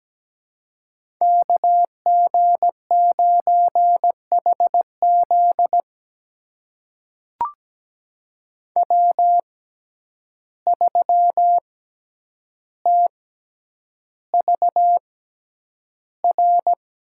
Morse code, heard as a keyed tone, about 17 wpm, spaced as for 9 wpm, 700 Hz. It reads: KG9HZ W3TVR
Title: USA Call Signs - 5 Characters - 6x Character Spacing - 17wpm